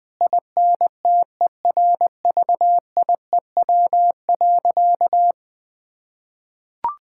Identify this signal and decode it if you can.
Morse code, heard as a keyed tone, about 20 wpm, 700 Hz. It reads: INTERVIEW.
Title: Sentences from Top 1600 Words 20wpm